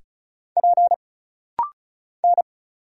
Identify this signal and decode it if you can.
Morse code, heard as a keyed tone, about 36 wpm, 700 Hz. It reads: P N
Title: Single Letters - Rapid-Fire 35wpm